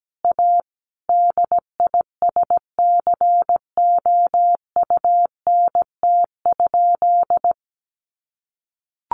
Morse code, heard as a keyed tone, about 17 wpm, 700 Hz. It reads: A DISCOUNT?
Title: Sentences from Top 1200 Words 17wpm